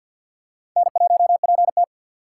Morse code, heard as a keyed tone, about 50 wpm, 700 Hz. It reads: N1PT
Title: USA Call Signs - 4 Characters - 2x Character Spacing - 50wpm